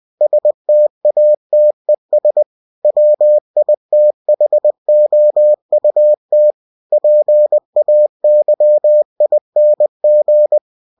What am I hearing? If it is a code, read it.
Morse code, heard as a keyed tone, about 20 wpm, 600 Hz. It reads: STATES WITHOUT PAYING
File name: winnie-20wpm